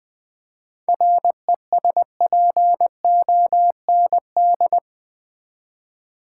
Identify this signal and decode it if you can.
Morse code, heard as a keyed tone, about 20 wpm, 700 Hz. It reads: RESPOND
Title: Seven Letter Words - 20wpm